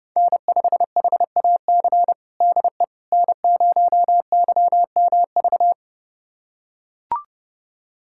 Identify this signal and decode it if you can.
Morse code, heard as a keyed tone, about 30 wpm, 700 Hz. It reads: N5HAC DE N0YMV
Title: USA Callsign DE Callsign - 5 Characters - 30wpm